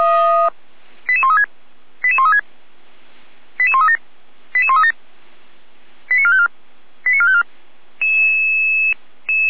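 Five-tone selective call (ZVEI), heard as quick runs of five tones, tones of about 80 ms. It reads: 88136 88136 88137 88137 89454 89454
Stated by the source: ZVEI five-tone selective call received on an SDR.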